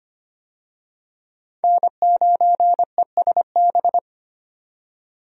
Morse code, heard as a keyed tone, about 25 wpm, 700 Hz. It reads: N9ESB
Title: USA Call Signs - 5 Characters - 25wpm